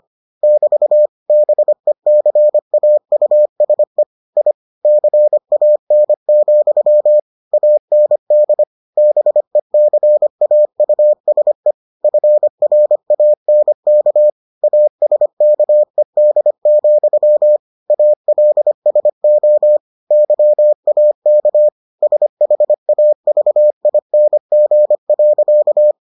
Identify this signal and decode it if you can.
Morse code, heard as a keyed tone, about 25 wpm, 600 Hz. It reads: <BT> BECAUSE I CAN, AND BECAUSE FRANK ASKED, ALSO YAK SHAVING.